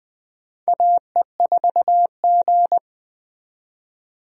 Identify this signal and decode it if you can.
Morse code, heard as a keyed tone, about 20 wpm, 700 Hz. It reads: AE4G
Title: USA Call Signs - 4 Characters - 20wpm